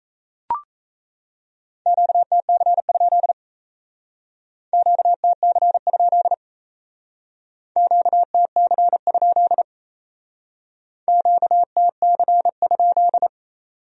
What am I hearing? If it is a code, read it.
Morse code, heard as a keyed tone, about 35 wpm, 700 Hz. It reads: QTC? QTC? QTC? QTC?